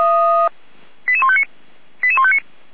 Five-tone selective call (ZVEI), heard as quick runs of five tones, tones of about 70 ms. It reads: 88159 88159